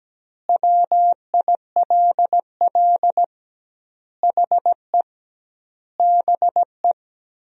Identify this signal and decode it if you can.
Morse code, heard as a keyed tone, about 17 wpm, 700 Hz. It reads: WILL HE BE
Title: 2x Word-Spacing - Sentences from Top 400 Words - 17wpm